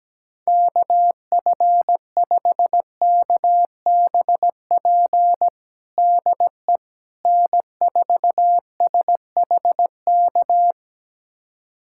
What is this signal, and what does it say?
Morse code, heard as a keyed tone, about 17 wpm, 700 Hz. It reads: KF5KBP DE N4SHK
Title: USA Callsign DE Callsign 17wpm